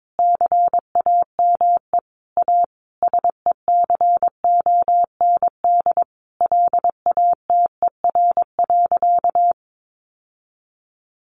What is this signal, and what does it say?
Morse code, heard as a keyed tone, about 22 wpm, 700 Hz. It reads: CAME A SECOND LATER.
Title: Sentences from Top 900 Words 22wpm